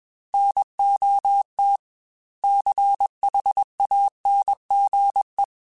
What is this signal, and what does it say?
Morse code, heard as a keyed tone, about 21 wpm, 800 Hz. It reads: NOT CHANGE
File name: Quotes / Quotes-mp3s / IgnoringFacts @20